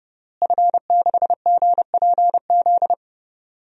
Morse code, heard as a keyed tone, about 31 wpm, 700 Hz. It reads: F6GPZ
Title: DXCC-100 Call Signs - 5 Characters - Repeated 3x - 30wpm